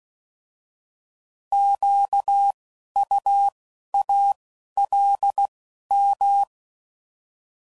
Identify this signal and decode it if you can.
Morse code, heard as a keyed tone, about 16 wpm, spaced as for 11 wpm, 800 Hz. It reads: QUALM